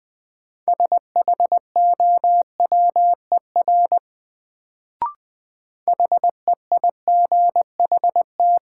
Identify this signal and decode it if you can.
Morse code, heard as a keyed tone, about 20 wpm, 700 Hz. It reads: SHOWER HEIGHT
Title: Sentences from Top 1600 Words - New Word Drill - 20wpm